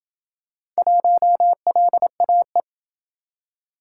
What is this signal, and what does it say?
Morse code, heard as a keyed tone, about 27 wpm, 700 Hz. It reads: 1LAE